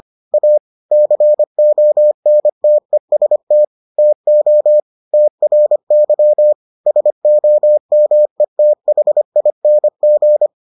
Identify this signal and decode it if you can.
Morse code, heard as a keyed tone, about 25 wpm, 600 Hz. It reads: A CONTEST TO TRY SOMETHING